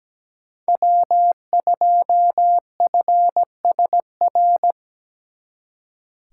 Morse code, heard as a keyed tone, about 20 wpm, 700 Hz. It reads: W2FSR
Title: Speed Racing - USA Call Signs 17wpm